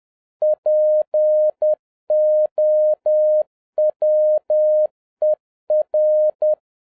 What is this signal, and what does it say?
Morse code, heard as a keyed tone, about 10 wpm, 610 Hz. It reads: POWER
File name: 2011-07-mp3 / n36 THE 3.11 2 10wpm